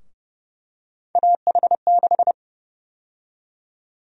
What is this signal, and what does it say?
Morse code, heard as a keyed tone, about 32 wpm, 700 Hz. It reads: AH6
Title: USA Call Sign Prefixes - Repeated 2x - 30wpm